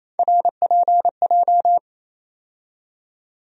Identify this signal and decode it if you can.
Morse code, heard as a keyed tone, about 28 wpm, 700 Hz. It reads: RPJ
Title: USA Call Sign Postfixes - 3 Letters 28wpm